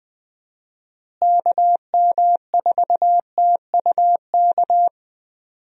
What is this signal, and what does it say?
Morse code, heard as a keyed tone, about 20 wpm, 700 Hz. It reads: KM4TUK
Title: USA Call Signs - 6 Characters - Repeated 3x - 20wpm